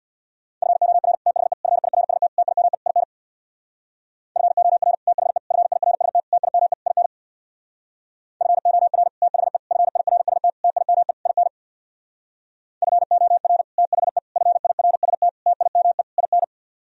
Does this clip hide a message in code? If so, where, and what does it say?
Yes: Morse code from 0.6 to 16.4 s, heard as a keyed tone, about 64 wpm, 700 Hz. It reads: FOR THE FIRST TIME IN FOR THE FIRST TIME IN FOR THE FIRST TIME IN FOR THE FIRST TIME IN